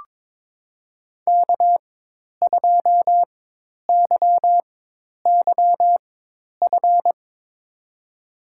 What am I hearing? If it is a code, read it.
Morse code, heard as a keyed tone, about 22 wpm, spaced as for 10 wpm, 700 Hz. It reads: K2YYF
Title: USA Call Signs - 5 Characters - 4x Character Spacing - 22wpm